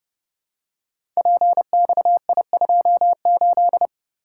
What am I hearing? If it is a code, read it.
Morse code, heard as a keyed tone, about 30 wpm, 700 Hz. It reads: PXI28
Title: Code Groups - 5 Letter:Numbers - 30wpm